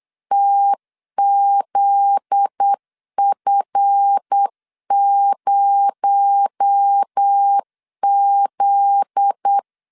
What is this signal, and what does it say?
Morse code, heard as a keyed tone, about 8 wpm, 800 Hz. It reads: TZF0Z
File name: LEZIONI / 7 GF9Z / 7 GF9Z 8wpm 3213